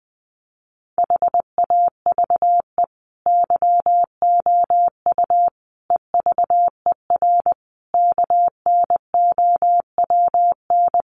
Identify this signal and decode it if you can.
Morse code, heard as a keyed tone, about 20 wpm, 700 Hz. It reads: HAVE YOU EVER KNOWN
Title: Sentences from Top 1400 Words 20wpm